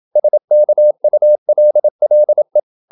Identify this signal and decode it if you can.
Morse code, heard as a keyed tone, about 27 wpm, 600 Hz. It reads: SKULLE
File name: Ferie QRQ 27wpm